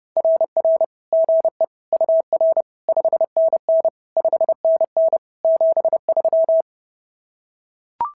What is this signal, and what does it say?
Morse code, heard as a keyed tone, about 30 wpm, 590 Hz. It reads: RR GE UR 5NN 5NN 73
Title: SOTA QSO Practice 30wpm